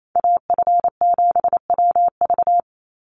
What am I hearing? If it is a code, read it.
Morse code, heard as a keyed tone, about 28 wpm, 700 Hz. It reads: AF7WV
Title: USA Call Signs 28wpm